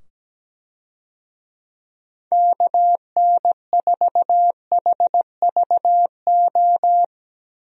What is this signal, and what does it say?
Morse code, heard as a keyed tone, about 17 wpm, 700 Hz. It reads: KN4HVO
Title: USA Call Signs - 6 Characters - 17wpm